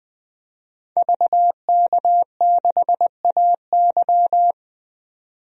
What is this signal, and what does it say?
Morse code, heard as a keyed tone, about 20 wpm, 700 Hz. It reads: VK6AY
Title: DXCC-100 Call Signs - 5 Characters - 20wpm